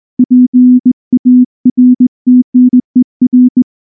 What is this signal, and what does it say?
Morse code, heard as a keyed tone, about 19 wpm, 250 Hz. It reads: PARTNER
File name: morse-training-20WPM